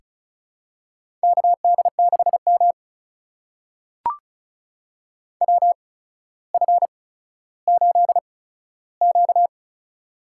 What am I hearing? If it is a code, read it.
Morse code, heard as a keyed tone, about 36 wpm, spaced as for 15 wpm, 700 Hz. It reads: KD6M WF8Q
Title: USA Call Signs - 4 Characters - 8x Character Spacing - 35wpm